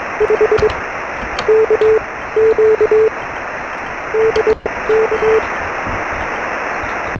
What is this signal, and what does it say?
Morse code, heard as a keyed tone, about 22 wpm, spaced as for 15 wpm, 410 Hz, with noise. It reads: 5 K Q DK
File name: sm7lcb / 0432 ly2va 13cm